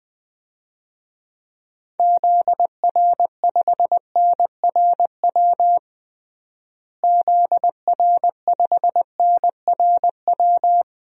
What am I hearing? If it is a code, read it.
Morse code, heard as a keyed tone, about 20 wpm, 700 Hz. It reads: ZR5NRW ZR5NRW
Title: DXCC-100 Call Signs - 6 Characters - Repeated 3x - 20wpm